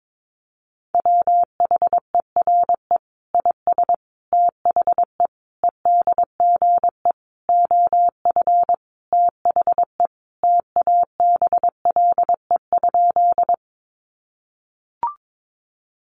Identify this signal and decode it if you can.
Morse code, heard as a keyed tone, about 22 wpm, 700 Hz. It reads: WHERE IS THE EDGE OF THE TABLE?
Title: Sentences from Top 1300 Words 22wpm